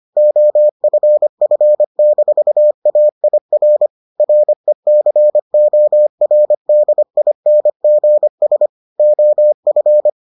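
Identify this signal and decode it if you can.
Morse code, heard as a keyed tone, about 25 wpm, 600 Hz. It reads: OFF-AIR RECORDINGS OF
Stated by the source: Morse edition of the Foundations of Amateur Radio episodes.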